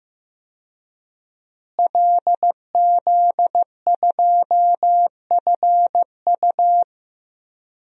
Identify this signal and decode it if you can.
Morse code, heard as a keyed tone, about 15 wpm, 700 Hz. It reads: LZ2FU
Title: DXCC-100 Call Signs - 5 Characters - Repeated 2x - 15wpm